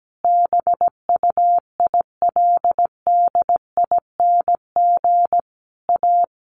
Morse code, heard as a keyed tone, about 17 wpm, 700 Hz. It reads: BUILDING A
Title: Sentences from Top 700 Words 17wpm